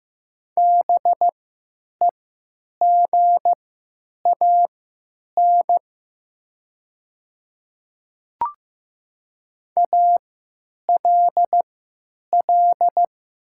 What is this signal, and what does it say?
Morse code, heard as a keyed tone, about 15 wpm, spaced as for 9 wpm, 700 Hz. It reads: BEGAN ALL
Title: 3x Character Spacing - Top 300 Words - 15wpm